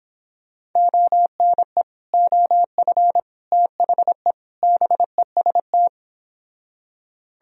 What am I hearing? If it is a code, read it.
Morse code, heard as a keyed tone, about 26 wpm, 700 Hz. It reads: ONE OF THE BEST